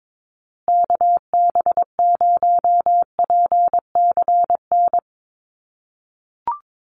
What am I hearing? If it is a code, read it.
Morse code, heard as a keyed tone, about 22 wpm, 700 Hz. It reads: KB0PCN
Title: USA Call Signs Repeated 2x 22wpm